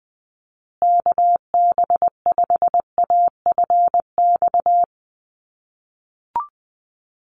Morse code, heard as a keyed tone, about 20 wpm, 700 Hz. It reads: KB5AFX
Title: USA Call Signs Repeated 3x 20wpm